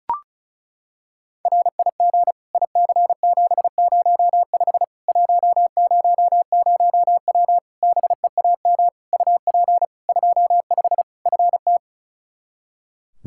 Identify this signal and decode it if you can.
Morse code, heard as a keyed tone, about 36 wpm, 700 Hz. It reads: RIG IC705 100W BEAM UP 25 FT
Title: K1USN SST QSO Practice 35wpm